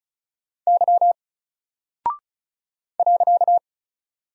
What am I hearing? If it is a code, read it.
Morse code, heard as a keyed tone, about 36 wpm, 700 Hz. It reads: Y .
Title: Single Character (Letter:Number:Punctuation:Prosign) 35wpm